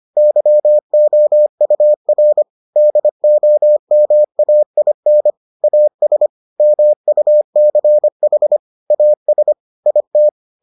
Morse code, heard as a keyed tone, about 25 wpm, 600 Hz. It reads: YOUR DOMAIN AS MUCH AS IT